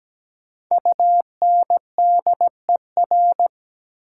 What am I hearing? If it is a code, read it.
Morse code, heard as a keyed tone, about 17 wpm, 700 Hz. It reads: UNDER
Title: Prefixes and Suffixes 17wpm